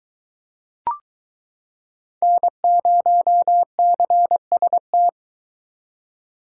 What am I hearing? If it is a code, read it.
Morse code, heard as a keyed tone, about 23 wpm, 700 Hz. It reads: N0CST